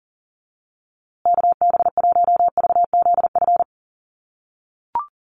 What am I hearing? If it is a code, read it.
Morse code, heard as a keyed tone, about 40 wpm, 700 Hz. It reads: KB1VZF
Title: USA Call Signs Repeated 2x 40wpm